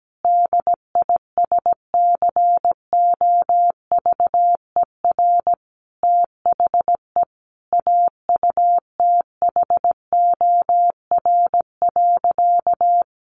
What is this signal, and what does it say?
Morse code, heard as a keyed tone, about 17 wpm, 700 Hz. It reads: DISCOVER THE AUTHOR.
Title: Sentences from Top 1200 Words 17wpm